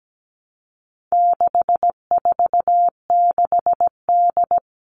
Morse code, heard as a keyed tone, about 17 wpm, 700 Hz. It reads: 646D
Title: All Cut Numbers - 4-Digit 17wpm